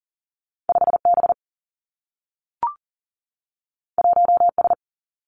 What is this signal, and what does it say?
Morse code, heard as a keyed tone, about 40 wpm, 700 Hz. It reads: 5 B 1 S
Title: Two Challenging Letter-Number Combinations 40wpm